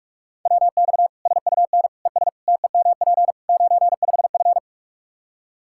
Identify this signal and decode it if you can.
Morse code, heard as a keyed tone, about 46 wpm, 700 Hz. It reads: WX SUN ES TEMP 95F
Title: Practice QSO - Exchange 2 - Info - 45wpm